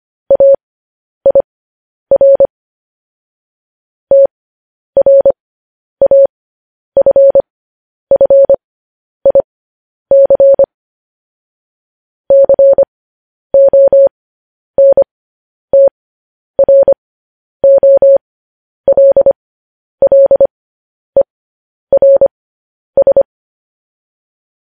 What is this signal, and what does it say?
Morse code, heard as a keyed tone, about 25 wpm, spaced as for 10 wpm, 550 Hz. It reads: AIR TRAFFIC CONTROLLERS
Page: News Headlines in Morse Code at 10 WPM